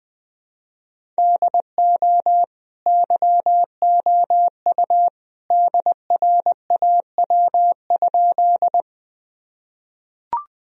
Morse code, heard as a keyed tone, about 20 wpm, 700 Hz. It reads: DO YOU DRAW?
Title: Sentences from Top 1000 Words - Review Part 1 - 20wpm